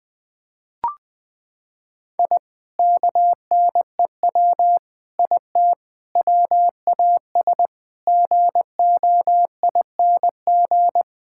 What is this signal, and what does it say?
Morse code, heard as a keyed tone, about 21 wpm, 700 Hz. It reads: I KNEW IT WAS GOING
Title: Sentences with 5-Word Phrases - 20wpm